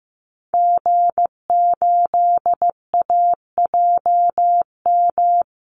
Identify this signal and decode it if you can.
Morse code, heard as a keyed tone, about 15 wpm, 700 Hz. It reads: G8AJM
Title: Call Signs Repeated 2x - 15wpm